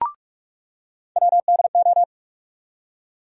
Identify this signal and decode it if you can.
Morse code, heard as a keyed tone, about 47 wpm, 700 Hz. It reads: WDO
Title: USA Call Sign Postfixes - 3 Letters 45wpm